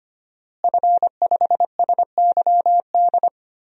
Morse code, heard as a keyed tone, about 25 wpm, 700 Hz. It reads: F5SYD
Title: DXCC-100 Call Signs - 5 Characters - Repeated 2x - 25wpm